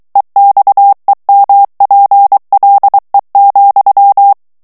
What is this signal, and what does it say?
Morse code, heard as a keyed tone, about 23 wpm, 800 Hz. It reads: EXEMPLE,
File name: morse mp3 / FRANCAIS TEXTE CLAIR 1 VITESSE 22 MOTS MINUTES